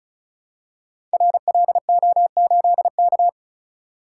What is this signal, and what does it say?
Morse code, heard as a keyed tone, about 36 wpm, 700 Hz. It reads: RLO8K
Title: Code Groups - 5 Letter:Numbers - 35wpm